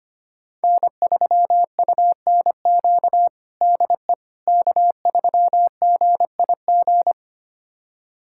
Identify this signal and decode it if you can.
Morse code, heard as a keyed tone, about 25 wpm, 700 Hz. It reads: N3UNQ DE K3GIG
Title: USA Callsign DE Callsign - 5 Characters - 25wpm